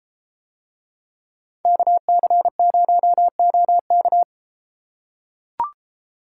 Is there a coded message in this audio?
Yes: Morse code from 1.7 to 4.2 s, heard as a keyed tone, about 32 wpm, 700 Hz. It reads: KC0OK